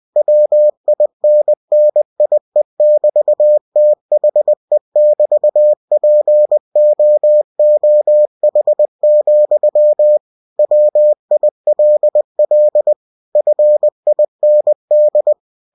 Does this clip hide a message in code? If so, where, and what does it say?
Yes: Morse code from 0.2 to 15.3 s, heard as a keyed tone, about 20 wpm, 600 Hz. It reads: WINNIE<BT>THE<BT>POOH, WILL FIND